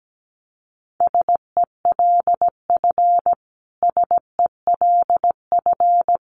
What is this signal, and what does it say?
Morse code, heard as a keyed tone, about 17 wpm, 700 Hz. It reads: SELF SELF
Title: Top 500 Words Repeated 17wpm